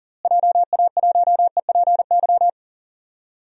Morse code, heard as a keyed tone, about 40 wpm, 700 Hz. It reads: JA1EPY
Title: DXCC-100 Call Signs - 6 Characters - Repeated 2x - 40wpm